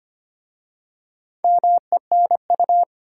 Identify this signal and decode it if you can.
Morse code, heard as a keyed tone, about 25 wpm, 700 Hz. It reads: MENU
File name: practice / 25wpm / Easy Words with Hints - 25wpm